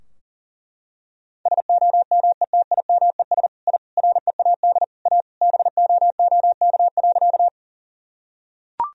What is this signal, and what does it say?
Morse code, heard as a keyed tone, about 40 wpm, 700 Hz. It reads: SOMETIMES I READ A BOOK.
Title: Sentences from Top 1000 Words - Review Part 3 - 40wpm